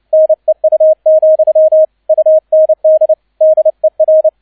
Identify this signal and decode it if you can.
Morse code, heard as a keyed tone, about 29 wpm, 630 Hz, with radio noise. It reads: NEU, UND DER
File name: 002 Zweiter Streich 30wpm